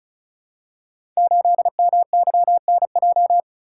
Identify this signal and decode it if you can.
Morse code, heard as a keyed tone, about 36 wpm, 700 Hz. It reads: 8MYNJ
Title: Code Groups - 5 Letter:Numbers - 35wpm